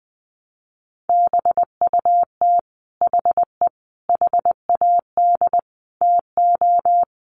Morse code, heard as a keyed tone, about 20 wpm, 700 Hz. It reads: BUT HE HAD TO
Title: Sentences from Top 100 Words 20wpm